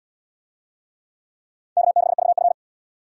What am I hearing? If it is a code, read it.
Morse code, heard as a keyed tone, about 69 wpm, 700 Hz. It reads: K6VF